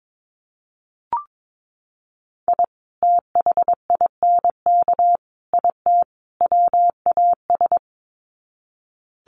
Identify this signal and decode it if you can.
Morse code, heard as a keyed tone, about 22 wpm, 700 Hz. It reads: I THINK IT WAS
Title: Sets of 4 Words 22wpm